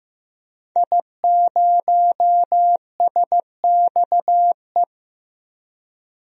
Morse code, heard as a keyed tone, about 15 wpm, 700 Hz. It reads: I0SXE
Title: DXCC-100 Call Signs - 5 Characters - 15wpm